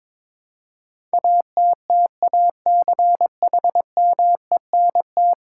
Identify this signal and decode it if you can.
Morse code, heard as a keyed tone, about 22 wpm, 700 Hz. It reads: ATTACHMENT
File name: practice / 22wpm / 10 Letter Words - First 3 Letters Given - 22wpm